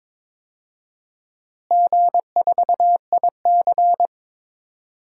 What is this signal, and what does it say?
Morse code, heard as a keyed tone, about 22 wpm, 700 Hz. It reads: G4IC
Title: DXCC-100 Call Signs - 4 Characters - Repeated 3x - 22wpm